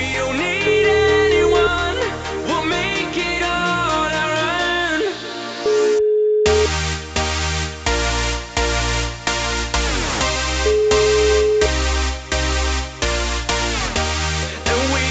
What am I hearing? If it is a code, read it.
Ringback tone (Europe, 425 Hz), with noise.